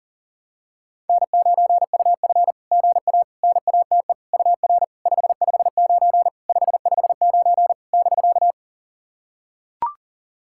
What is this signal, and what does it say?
Morse code, heard as a keyed tone, about 42 wpm, 690 Hz. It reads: N9UF GA NATE UR 559 559 <BK>
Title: SOTA QSO Practice 40wpm